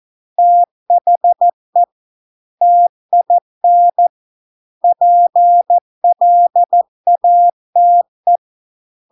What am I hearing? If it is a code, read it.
Morse code, heard as a keyed tone, about 14 wpm, 700 Hz. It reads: THE TIN PLATE